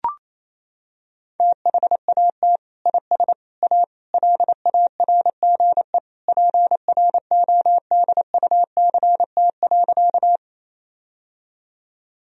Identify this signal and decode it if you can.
Morse code, heard as a keyed tone, about 28 wpm, 700 Hz. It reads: THAT IS A LARGE PRODUCT.
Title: Sentences from Top 1000 Words - Review Part 2 - 28wpm